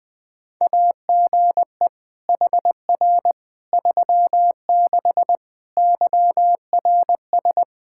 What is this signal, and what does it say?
Morse code, heard as a keyed tone, about 20 wpm, 700 Hz. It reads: AGE HR 36 YRS
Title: Practice QSO - Exchange 3 - More Info - 20wpm